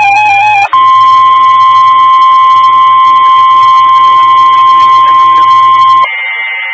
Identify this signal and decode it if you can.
Emergency Alert System (SAME) tones (the 1050 Hz attention tone). It is heard from 0.7 to 6.1 s.
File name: W5NGU Denton County ARC and Skywarn 146920 Mhz Repeater